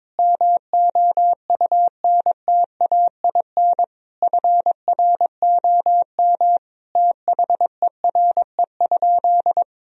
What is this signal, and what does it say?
Morse code, heard as a keyed tone, about 22 wpm, 700 Hz. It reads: MOUNTAIN FROM THERE?
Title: Sentences from Top 1000 Words - Review Part 1 - 22wpm